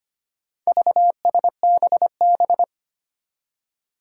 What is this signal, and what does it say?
Morse code, heard as a keyed tone, about 25 wpm, 700 Hz. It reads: VSBB
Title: Four Challenging Letter-Number Combinations 25wpm